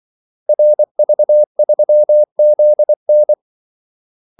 Morse code, heard as a keyed tone, about 24 wpm, 600 Hz. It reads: RV3ZN
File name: Callsigns-24wpm